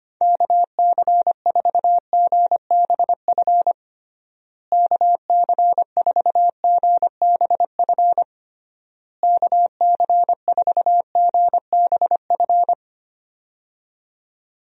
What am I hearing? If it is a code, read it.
Morse code, heard as a keyed tone, about 25 wpm, 700 Hz. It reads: KC4GBF KC4GBF KC4GBF
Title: USA Call Signs - 6 Characters - Repeated 3x - 25wpm